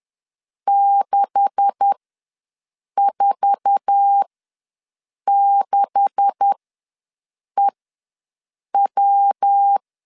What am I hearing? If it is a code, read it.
Morse code, heard as a keyed tone, about 10 wpm, spaced as for 6 wpm, 800 Hz. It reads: 646EW
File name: LEZIONI / 9 286 ws 10wpm 4032